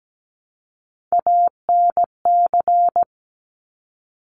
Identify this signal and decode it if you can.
Morse code, heard as a keyed tone, about 17 wpm, 690 Hz. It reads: ANC
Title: Top Three Letter Combinations in Words 17wpm